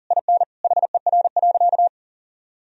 Morse code, heard as a keyed tone, about 40 wpm, 700 Hz. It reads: IN HER.
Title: Sentences from Top 1000 Words - Review Part 4 - 40wpm